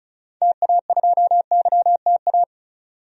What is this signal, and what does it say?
Morse code, heard as a keyed tone, about 36 wpm, 700 Hz. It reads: TA2YTA
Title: DXCC-100 Call Signs - 6 Characters - 35wpm